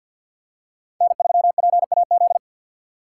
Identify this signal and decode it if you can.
Morse code, heard as a keyed tone, about 52 wpm, 700 Hz. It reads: N3PAZ
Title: USA Call Signs - 5 Characters - 6x Character Spacing - 50wpm